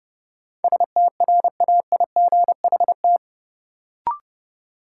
Morse code, heard as a keyed tone, about 32 wpm, 700 Hz. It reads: STRAIGHT
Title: Sentences from Top 1000 Words - New Word Drill - 30wpm